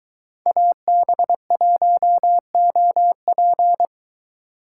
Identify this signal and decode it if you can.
Morse code, heard as a keyed tone, about 23 wpm, 700 Hz. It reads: AB1OP